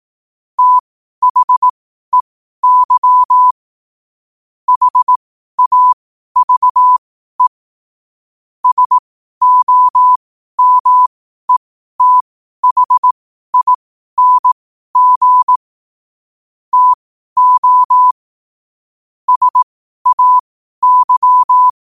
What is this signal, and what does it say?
Morse code, heard as a keyed tone, about 18 wpm, spaced as for 12 wpm, 1000 Hz. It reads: THEY HAVE SOMETHING TO SAY
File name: qotdfiles / 12wpm